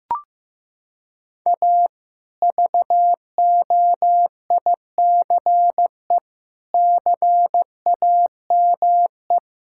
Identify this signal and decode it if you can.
Morse code, heard as a keyed tone, about 16 wpm, 700 Hz. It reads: A VOICE CAME
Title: Sentences from Top 1000 Words - Review Part 5 - 15wpm